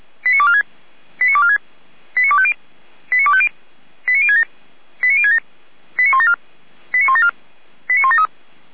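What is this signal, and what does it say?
Five-tone selective call (ZVEI), heard as quick runs of five tones, tones of about 70 ms. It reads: 89246 89246 89250 89250 89067 89067 89174 89174 89173
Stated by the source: ZVEI five-tone selective call received by SDR, from the Feuerwehr Orlamünde audio server.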